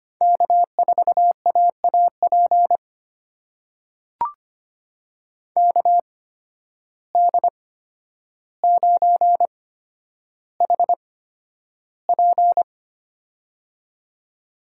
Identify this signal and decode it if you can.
Morse code, heard as a keyed tone, about 25 wpm, spaced as for 11 wpm, 700 Hz. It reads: K4AAP KD9HP
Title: USA Call Signs - 5 Characters - 8x Character Spacing - 25wpm